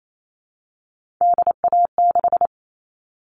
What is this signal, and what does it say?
Morse code, heard as a keyed tone, about 28 wpm, 700 Hz. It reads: DA6
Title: All Cut Numbers - 3-Digit 28wpm